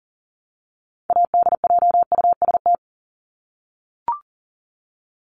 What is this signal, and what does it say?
Morse code, heard as a keyed tone, about 40 wpm, 700 Hz. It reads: ADJUST
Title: Top 1501-1750 Words 40wpm